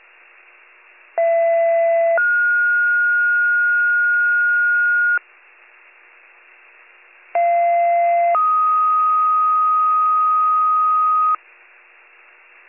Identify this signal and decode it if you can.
Two-tone page, heard as a one-second tone, then a three-second tone, about 670 then 1430 Hz, about 690 then 1250 Hz. It is heard from 1.2 to 11.4 s.